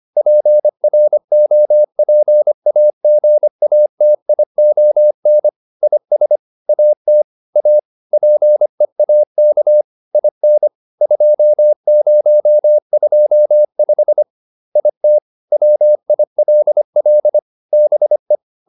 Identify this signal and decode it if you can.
Morse code, heard as a keyed tone, about 25 wpm, 600 Hz. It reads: PROPAGATION IS AT A PEAK IN 2025 IT WILL BE
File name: SS311 25